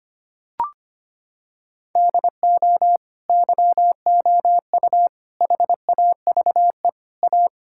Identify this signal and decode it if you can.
Morse code, heard as a keyed tone, about 25 wpm, 700 Hz. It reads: DO YOU HAVE A
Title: Sentences from Top 1600 Words 25wpm